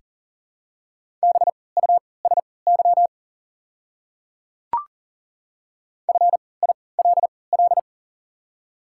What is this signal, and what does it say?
Morse code, heard as a keyed tone, about 40 wpm, spaced as for 23 wpm, 700 Hz. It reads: BUSY FILL
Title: 3x Character Spacing - Top 500 Words - 40wpm